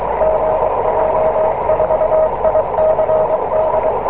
Morse code, heard as a keyed tone, about 23 wpm, 630 Hz, with noise. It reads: GJ4ICD